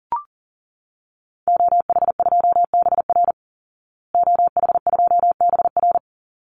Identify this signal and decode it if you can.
Morse code, heard as a keyed tone, about 40 wpm, 700 Hz. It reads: OH2BR OH2BR
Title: Call Signs Repeated 3x - 40wpm